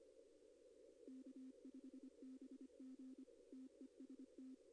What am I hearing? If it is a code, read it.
Morse code, heard as a keyed tone, about 25 wpm, 260 Hz. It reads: K5BG TEST
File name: contests / arrl-160-k5tr-op-w5tn -2200